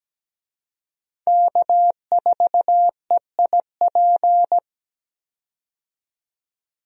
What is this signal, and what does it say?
Morse code, heard as a keyed tone, about 17 wpm, 700 Hz. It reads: K4EIP
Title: USA Call Signs - 5 Characters - 17wpm